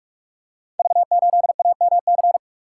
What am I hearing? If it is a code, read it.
Morse code, heard as a keyed tone, about 44 wpm, 690 Hz. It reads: V8AMC
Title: Code Groups - 5 Letter:Numbers - 45wpm